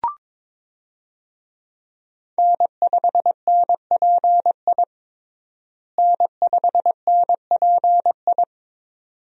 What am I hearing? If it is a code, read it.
Morse code, heard as a keyed tone, about 21 wpm, 700 Hz. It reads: N5NPI N5NPI
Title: USA Call Signs - 5 Characters - Repeated 2x - 22wpm